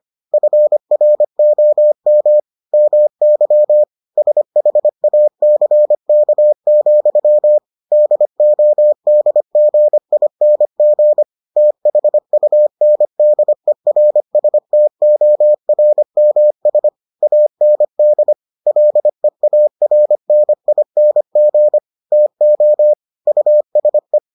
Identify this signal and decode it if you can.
Morse code, heard as a keyed tone, about 25 wpm, 600 Hz. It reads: FROM MY SHACK, DODGING THUNDERSTORMS AND LEARNING TO USE